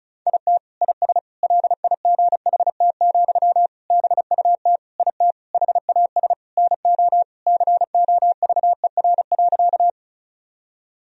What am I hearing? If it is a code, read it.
Morse code, heard as a keyed tone, about 36 wpm, 700 Hz. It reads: IT IS LIGHT, BUT IT HAS NO COVER.
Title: Sentences from Top 1000 Words - Review Part 1 - 35wpm